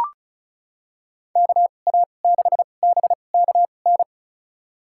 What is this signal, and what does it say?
Morse code, heard as a keyed tone, about 36 wpm, spaced as for 25 wpm, 700 Hz. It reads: KA6BKN
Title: USA Call Signs - 2x Character Spacing - 35wpm